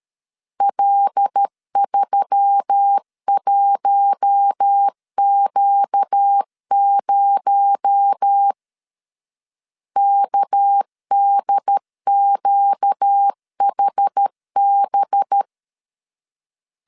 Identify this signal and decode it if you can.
Morse code, heard as a keyed tone, about 13 wpm, 800 Hz. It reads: L31Q0 KDQHB
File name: CW / DET / 1 6 1500C S B 12wpm 3431